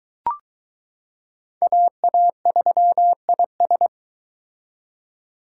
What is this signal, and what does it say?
Morse code, heard as a keyed tone, about 22 wpm, 700 Hz. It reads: AA3IS